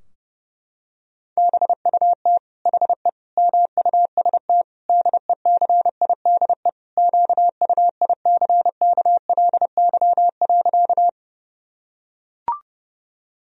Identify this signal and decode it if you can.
Morse code, heard as a keyed tone, about 30 wpm, 700 Hz. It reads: BUT HE MUST DECIDE QUICKLY.
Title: Sentences from Top 1000 Words - Review Part 3 - 30wpm